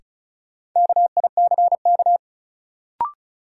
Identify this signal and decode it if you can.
Morse code, heard as a keyed tone, about 36 wpm, 700 Hz. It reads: KICK
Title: Sentences from Top 1300 Words - New Word Drill - 35wpm